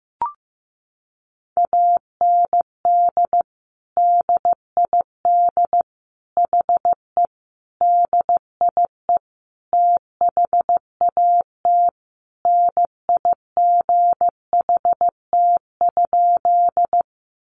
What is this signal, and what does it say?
Morse code, heard as a keyed tone, about 15 wpm, 700 Hz. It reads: AND DID HE DIE THAT NIGHT?
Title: Sentences from Top 1300 Words 15wpm